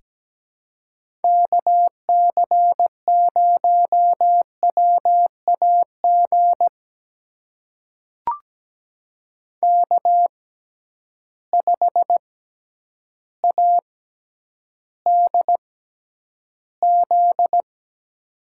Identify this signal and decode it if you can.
Morse code, heard as a keyed tone, about 17 wpm, spaced as for 9 wpm, 700 Hz. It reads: KC0WAG K5ADZ
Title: USA Call Signs - 6x Character Spacing - 17wpm